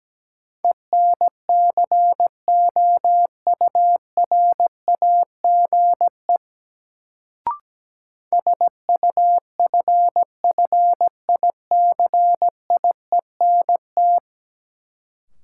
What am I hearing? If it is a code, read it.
Morse code, heard as a keyed tone, about 17 wpm, 700 Hz. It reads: ENCOURAGE SUFFICIENT
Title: Sentences from Top 1400 Words - New Word Drill - 17wpm